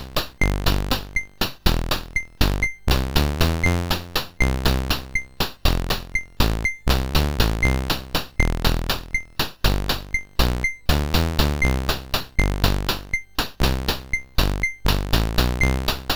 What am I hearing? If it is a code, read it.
There is no code.